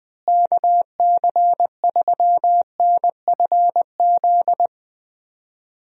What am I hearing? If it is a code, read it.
Morse code, heard as a keyed tone, about 20 wpm, 700 Hz. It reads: KC3NFZ
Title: USA Call Signs - 6 Characters - Repeated 3x - 20wpm